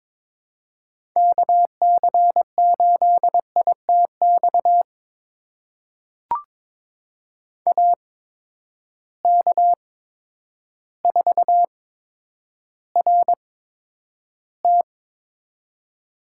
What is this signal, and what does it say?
Morse code, heard as a keyed tone, about 22 wpm, spaced as for 10 wpm, 700 Hz. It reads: KC8ITX AK4RT
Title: USA Call Signs - 8x Character Spacing - 22wpm